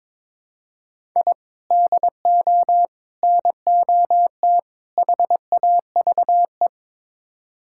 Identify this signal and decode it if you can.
Morse code, heard as a keyed tone, about 22 wpm, 700 Hz. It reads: I DO NOT HAVE
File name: practice / 22wpm / Build A Phrase - 5-Word Phrases - 22wpm